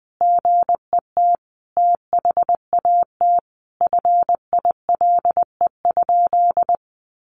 Morse code, heard as a keyed tone, about 20 wpm, 700 Hz. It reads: GET THAT FILE?
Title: Sentences from Top 1200 Words 20wpm